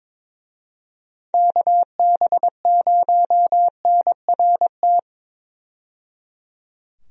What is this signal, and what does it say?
Morse code, heard as a keyed tone, about 21 wpm, 700 Hz. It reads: KB0NRT
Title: USA Call Signs - 6 Characters - 22wpm